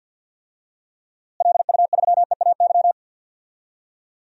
Morse code, heard as a keyed tone, about 51 wpm, 700 Hz. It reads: LU3EAY
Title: DXCC-100 Call Signs - 6 Characters - Repeated 3x - 50wpm